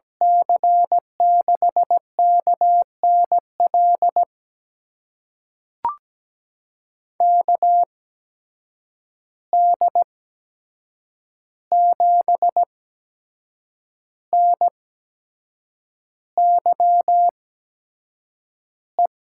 Morse code, heard as a keyed tone, about 17 wpm, spaced as for 7 wpm, 700 Hz. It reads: C6KNL KD7NYE
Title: USA Call Signs - 6 Characters - 8x Character Spacing - 17wpm